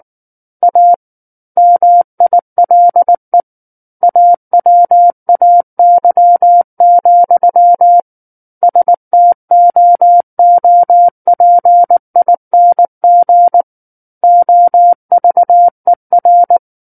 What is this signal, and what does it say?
Morse code, heard as a keyed tone, about 19 wpm, 700 Hz. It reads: A MILE AWAY, STOOPING OVER
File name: Book One / 10-Chapter 10 - 19 WPM